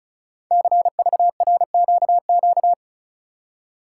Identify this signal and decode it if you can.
Morse code, heard as a keyed tone, about 36 wpm, 690 Hz. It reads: CVRQQ
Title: Code Groups - 5 Letters - 35wpm